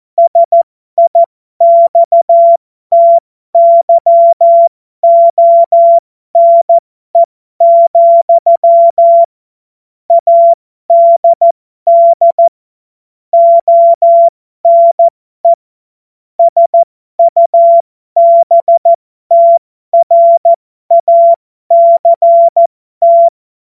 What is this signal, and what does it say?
Morse code, heard as a keyed tone, about 14 wpm, spaced as for 12 wpm, 680 Hz. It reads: SIXTYONE, ADD ONE SUBTRACT
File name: Morsekurs / Witze